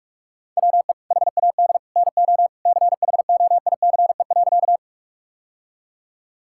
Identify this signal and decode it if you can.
Morse code, heard as a keyed tone, about 46 wpm, 700 Hz. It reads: WE HAD NO CHOICE.
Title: Sentences from Top 1000 Words - Review Part 6 - 45wpm